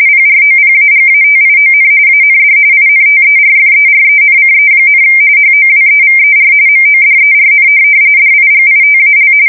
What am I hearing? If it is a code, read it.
RTTY, about 45.45 baud. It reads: PROPAGATION AND INTERFERENCE, HOWEVER MODERN DIGITAL
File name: fsk170 / teletype-RTTY